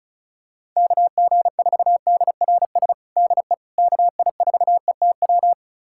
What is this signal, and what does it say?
Morse code, heard as a keyed tone, about 34 wpm, 700 Hz. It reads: KG4DRS DE KI4ETW
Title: USA Callsign DE Callsign - 6 Characters - 35wpm